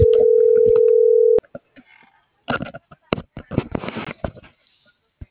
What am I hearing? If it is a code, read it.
Ringback tone (North America), 440 + 480 Hz.